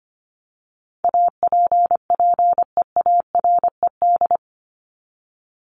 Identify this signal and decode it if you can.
Morse code, heard as a keyed tone, about 25 wpm, 700 Hz. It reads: APPEARED
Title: ED Suffix - Top 100 Words with Base Words 25wpm